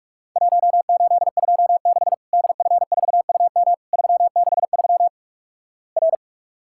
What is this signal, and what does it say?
Morse code, heard as a keyed tone, about 44 wpm, 700 Hz. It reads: 1826 DF4UM 363 R
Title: Worked All Europe DX Contest - QTC Exchange - 45wpm